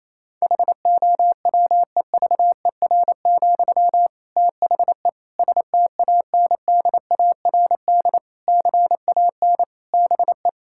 Morse code, heard as a keyed tone, about 28 wpm, 690 Hz. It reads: HOWEVER, THE STANDARD CAN BE
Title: Sentences from Top 1000 Words - Review Part 6 - 28wpm